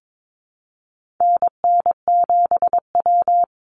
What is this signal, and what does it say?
Morse code, heard as a keyed tone, about 22 wpm, 700 Hz. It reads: NN7W
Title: Call Signs Repeated 3x - 22wpm